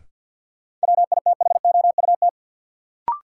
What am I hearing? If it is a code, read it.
Morse code, heard as a keyed tone, about 51 wpm, 700 Hz. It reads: WITHOUT
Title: Sentences from Top 600 Words - New Word Drill - 50wpm